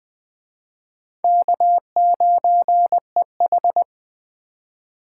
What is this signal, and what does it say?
Morse code, heard as a keyed tone, about 24 wpm, 700 Hz. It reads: K9EH
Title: Speed Racing - USA 1x2 Call Signs 20wpm